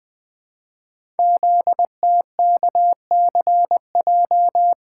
Morse code, heard as a keyed tone, about 20 wpm, 700 Hz. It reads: ZTKCJ
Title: Code Groups - 5 Letters - 20wpm